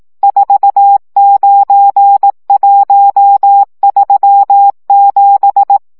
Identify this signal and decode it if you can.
Morse code, heard as a keyed tone, about 18 wpm, 800 Hz. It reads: 49137
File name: morze vizsga